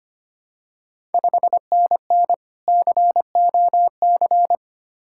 Morse code, heard as a keyed tone, about 25 wpm, 700 Hz. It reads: 5NN COC
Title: Ontario QSO Party Exchange 25wpm